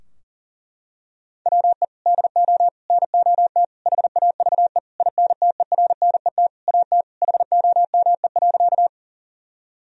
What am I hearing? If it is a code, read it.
Morse code, heard as a keyed tone, about 40 wpm, 700 Hz. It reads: WE DO NOT HAVE INTERNET AT HOME.
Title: Sentences from Top 1000 Words - Review Part 4 - 40wpm